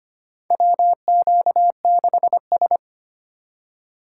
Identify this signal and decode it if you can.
Morse code, heard as a keyed tone, about 25 wpm, 700 Hz. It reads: WQ6S
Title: USA Call Signs - 4 Characters - 3x Character Spacing - 25wpm